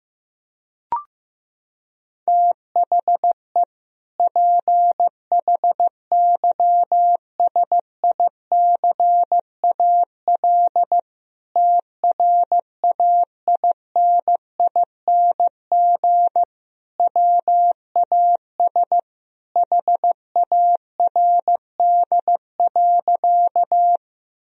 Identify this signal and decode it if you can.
Morse code, heard as a keyed tone, about 15 wpm, 700 Hz. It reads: THE PHYSICAL TRAINING WAS HARD.
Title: Sentences from Top 1000 Words - Review Part 4 - 15wpm